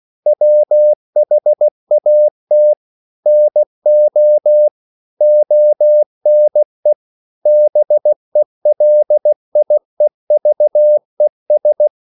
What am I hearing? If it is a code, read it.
Morse code, heard as a keyed tone, about 16 wpm, 600 Hz. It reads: WHAT NO ONE BELIEVES
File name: CW / Practice / EnglishQuotations0416